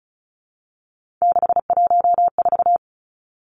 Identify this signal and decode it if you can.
Morse code, heard as a keyed tone, about 36 wpm, 700 Hz. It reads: 614
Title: Three Challenging Letter-Number Combinations 35wpm